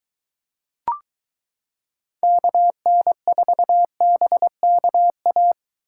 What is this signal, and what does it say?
Morse code, heard as a keyed tone, about 22 wpm, 700 Hz. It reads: KN4BKA